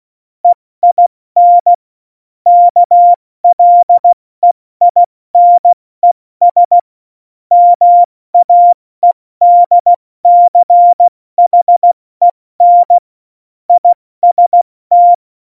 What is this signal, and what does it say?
Morse code, heard as a keyed tone, about 16 wpm, 700 Hz. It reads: EIN KLEINES MAEDCHEN IST
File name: Morsekurs / Witze / Witz 01. August 17, Tempo 70 F80